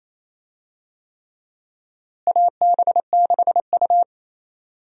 Morse code, heard as a keyed tone, about 28 wpm, 700 Hz. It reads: AB6U
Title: USA Call Signs - 4 Characters - Repeated 3x - 28wpm